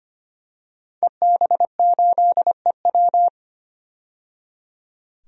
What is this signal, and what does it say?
Morse code, heard as a keyed tone, about 25 wpm, 700 Hz. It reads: EB8EW
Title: DXCC-100 Call Signs - 5 Characters - 25wpm